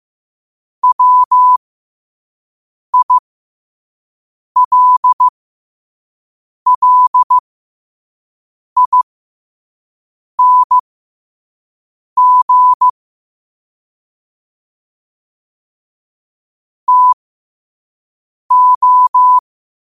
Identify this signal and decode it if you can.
Morse code, heard as a keyed tone, about 15 wpm, spaced as for 5 wpm, 1000 Hz. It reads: WILLING TO
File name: qotdfiles / 5wpm